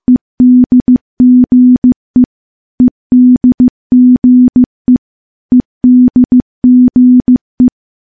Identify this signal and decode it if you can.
Morse code, heard as a keyed tone, about 15 wpm, 250 Hz. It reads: EDGE EDGE EDGE
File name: morse-training-15WPM